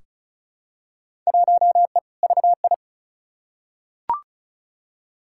Morse code, heard as a keyed tone, about 36 wpm, 700 Hz. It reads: 1E VI
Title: Field Day Exchanges Encore - 35wpm